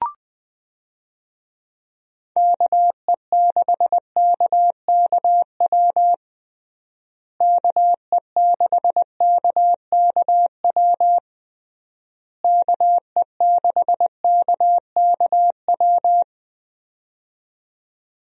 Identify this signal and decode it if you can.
Morse code, heard as a keyed tone, about 21 wpm, 700 Hz. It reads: KE6KKW KE6KKW KE6KKW
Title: USA Call Signs - 6 Characters - Repeated 3x - 20wpm